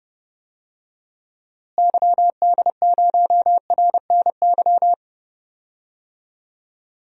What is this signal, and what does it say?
Morse code, heard as a keyed tone, about 30 wpm, 700 Hz. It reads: YD0RNY
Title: DXCC-100 Call Signs - 6 Characters - Repeated 3x - 30wpm